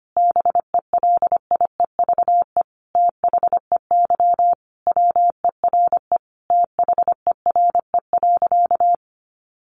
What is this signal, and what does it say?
Morse code, heard as a keyed tone, about 25 wpm, 700 Hz. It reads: BELIEVE THEY WERE THERE.
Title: Sentences from Top 1000 Words 25wpm